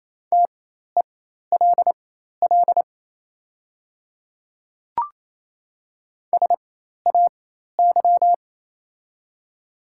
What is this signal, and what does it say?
Morse code, heard as a keyed tone, about 28 wpm, spaced as for 13 wpm, 700 Hz. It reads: TELL SAY
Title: 4x Character Spacing - Top 200 Words - 28wpm